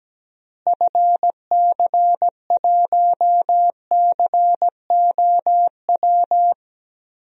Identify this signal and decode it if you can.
Morse code, heard as a keyed tone, about 17 wpm, 700 Hz. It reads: FC1COW
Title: DXCC-100 Call Signs - 6 Characters - Repeated 3x - 17wpm